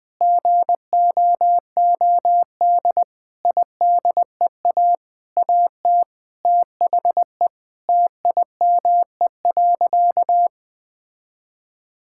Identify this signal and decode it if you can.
Morse code, heard as a keyed tone, about 20 wpm, 700 Hz. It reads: GOOD IDEA AT THE TIME.
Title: Sentences with 9-Word Phrases - 20wpm